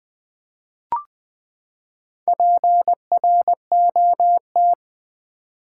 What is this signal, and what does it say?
Morse code, heard as a keyed tone, about 20 wpm, 700 Hz. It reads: PROT
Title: Build A Word - 4 Letters At A Time - 8-Letter Words - 20wpm